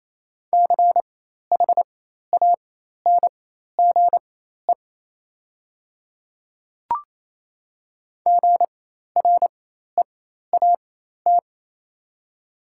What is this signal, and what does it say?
Morse code, heard as a keyed tone, about 28 wpm, spaced as for 13 wpm, 700 Hz. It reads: CHANGE GREAT
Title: 4x Character Spacing - Top 200 Words - 28wpm